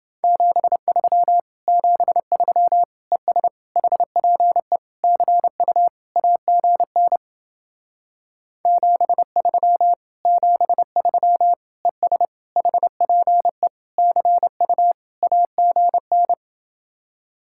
Morse code, heard as a keyed tone, about 30 wpm, 700 Hz. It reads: 73 73 ES HPE CU AGN 73 73 ES HPE CU AGN